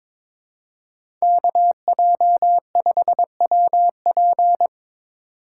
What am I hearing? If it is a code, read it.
Morse code, heard as a keyed tone, about 22 wpm, 700 Hz. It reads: KJ5WP
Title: USA Call Signs - 5 Characters - 3x Character Spacing - 22wpm